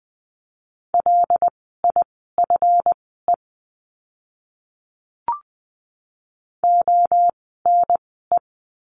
Morse code, heard as a keyed tone, about 20 wpm, spaced as for 14 wpm, 700 Hz. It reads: LIFE ONE
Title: 230 Lesson 37f - Letter x - Words 20wpm